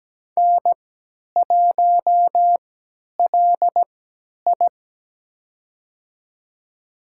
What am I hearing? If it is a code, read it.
Morse code, heard as a keyed tone, about 17 wpm, spaced as for 10 wpm, 700 Hz. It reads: N1LI
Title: USA Call Signs - 4 Characters - 3x Character Spacing - 17wpm